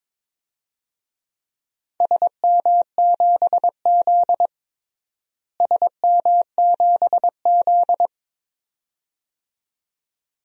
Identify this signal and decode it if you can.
Morse code, heard as a keyed tone, about 21 wpm, 700 Hz. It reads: SM7Z SM7Z
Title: DXCC-100 Call Signs - 4 Characters - Repeated 2x - 22wpm